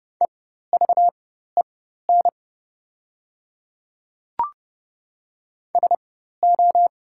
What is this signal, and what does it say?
Morse code, heard as a keyed tone, about 30 wpm, spaced as for 14 wpm, 700 Hz. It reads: EVEN SO
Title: 4x Character Spacing - Top 200 Words - 30wpm